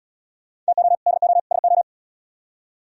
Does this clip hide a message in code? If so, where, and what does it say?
Yes: Morse code from 0.7 to 1.8 s, heard as a keyed tone, about 77 wpm, 700 Hz. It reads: TO DO SO